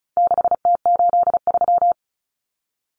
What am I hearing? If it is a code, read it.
Morse code, heard as a keyed tone, about 34 wpm, 700 Hz. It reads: 6T83
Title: Common Cut Numbers - 4-Digit 35wpm